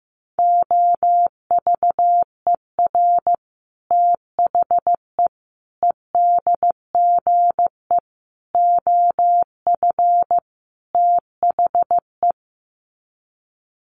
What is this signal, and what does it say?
Morse code, heard as a keyed tone, about 15 wpm, 700 Hz. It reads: OVER THE EDGE OF THE
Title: Sets of 5 Words Encore2 15wpm